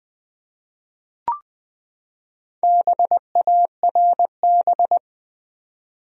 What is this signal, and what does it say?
Morse code, heard as a keyed tone, about 20 wpm, 700 Hz. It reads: BARB
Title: USA Common Names - Max 5 Letters - 20wpm